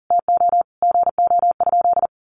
Morse code, heard as a keyed tone, about 40 wpm, 700 Hz. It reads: TO GO?
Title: Sentences from Top 1100 Words 40wpm